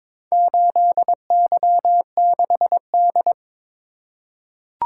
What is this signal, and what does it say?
Morse code, heard as a keyed tone, about 22 wpm, 700 Hz. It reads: 8Y6D
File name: practice / 22wpm / Four Letter-Number Combinations - 22wpm